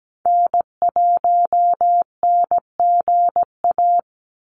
Morse code, heard as a keyed tone, about 17 wpm, 700 Hz. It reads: N1NGA
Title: USA Call Signs 17wpm